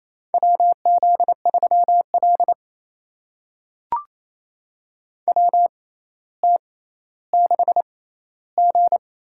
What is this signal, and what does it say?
Morse code, heard as a keyed tone, about 28 wpm, spaced as for 14 wpm, 700 Hz. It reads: WZ3L WT6G
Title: USA Call Signs - 4 Characters - 6x Character Spacing - 28wpm